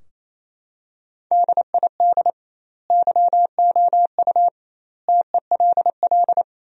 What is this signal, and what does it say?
Morse code, heard as a keyed tone, about 28 wpm, 700 Hz. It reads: DID YOU TELL
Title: 2x Word-Spacing - Sentences from Top 400 Words - 28wpm